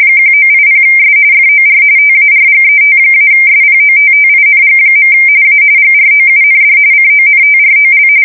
RTTY, about 45.45 baud. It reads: 1980S FOR TARGET-SPOTTING. A GROUND CONTROL
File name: fsk170 / USS-New-Jersey